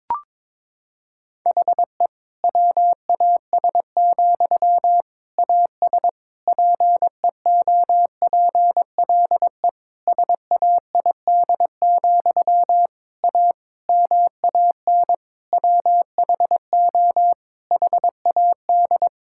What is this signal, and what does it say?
Morse code, heard as a keyed tone, about 22 wpm, 700 Hz. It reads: HE WAS, AS PEOPLE SAID, A MAN WHO HAD
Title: Sentences from Top 1000 Words - Review Part 6 - 22wpm